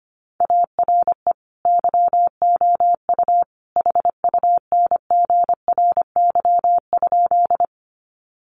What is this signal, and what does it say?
Morse code, heard as a keyed tone, about 25 wpm, 700 Hz. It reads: ARE YOU HUNGRY?
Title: Sentences from Top 1400 Words 25wpm